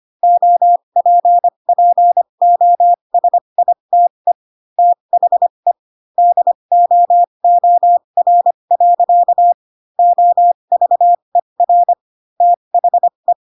Morse code, heard as a keyed tone, about 25 wpm, spaced as for 22 wpm, 700 Hz. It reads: OPPOSITE THE DOOR. OVER THE
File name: Pinocchio / Pinocchio ch3 18 25 ik1hge